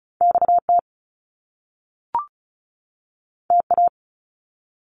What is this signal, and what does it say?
Morse code, heard as a keyed tone, about 36 wpm, 700 Hz. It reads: XT TA
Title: Top Two Letter Combinations in Words 35wpm